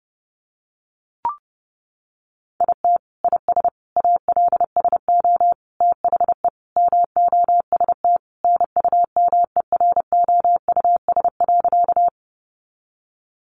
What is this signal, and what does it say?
Morse code, heard as a keyed tone, about 30 wpm, 700 Hz. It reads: IT IS ALSO THE MOST NUMEROUS.
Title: Sentences from Top 1400 Words 30wpm